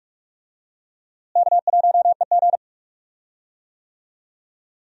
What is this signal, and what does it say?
Morse code, heard as a keyed tone, about 41 wpm, 700 Hz. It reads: K1EG
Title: USA Call Signs - 4 Characters - Repeated 3x - 45wpm